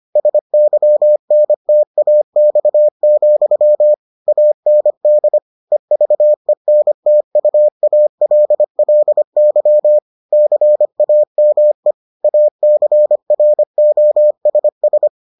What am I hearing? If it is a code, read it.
Morse code, heard as a keyed tone, about 25 wpm, 600 Hz. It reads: SYNTAX, AND EVENTUALLY CAME ACROSS